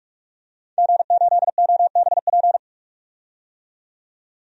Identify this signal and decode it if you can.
Morse code, heard as a keyed tone, about 44 wpm, 690 Hz. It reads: G8OBP
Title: DXCC-100 Call Signs - 5 Characters - 45wpm